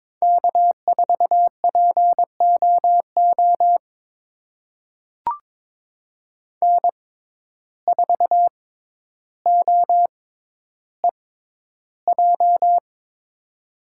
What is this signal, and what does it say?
Morse code, heard as a keyed tone, about 21 wpm, spaced as for 11 wpm, 700 Hz. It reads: K4POO N4OEJ
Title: USA Call Signs - 5 Characters - 6x Character Spacing - 22wpm